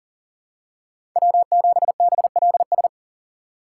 Morse code, heard as a keyed tone, about 40 wpm, 700 Hz. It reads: W7BLS
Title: USA Call Signs - 5 Characters - 2x Character Spacing - 40wpm